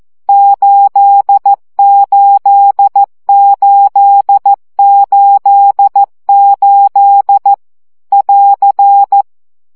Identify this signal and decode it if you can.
Morse code, heard as a keyed tone, about 14 wpm, 800 Hz. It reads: 88888 <AR>
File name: CW gyakorlas 8 12 baud